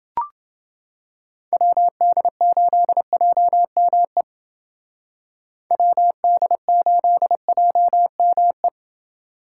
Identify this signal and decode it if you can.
Morse code, heard as a keyed tone, about 29 wpm, 700 Hz. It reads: WD8JME WD8JME